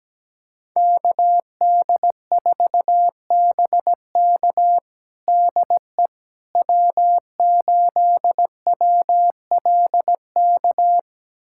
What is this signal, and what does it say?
Morse code, heard as a keyed tone, about 17 wpm, 700 Hz. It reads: KD4BK DE W8WLK
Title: USA Callsign DE Callsign 17wpm